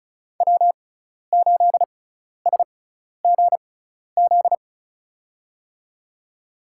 Morse code, heard as a keyed tone, about 34 wpm, spaced as for 12 wpm, 700 Hz. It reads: W8SGZ
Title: USA Call Signs - 5 Characters - 6x Character Spacing - 35wpm